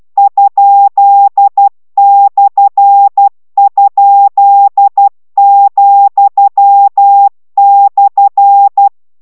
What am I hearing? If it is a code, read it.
Morse code, heard as a keyed tone, about 12 wpm, 800 Hz. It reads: ?/?,/